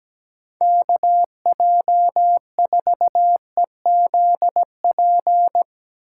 Morse code, heard as a keyed tone, about 17 wpm, 700 Hz. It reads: KJ4EZP
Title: USA Call Signs - 6 Characters - 8x Character Spacing - 17wpm